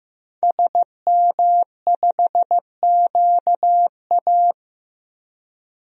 Morse code, heard as a keyed tone, about 15 wpm, 700 Hz. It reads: SM5QA
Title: DXCC-100 Call Signs - 5 Characters - Repeated 2x - 15wpm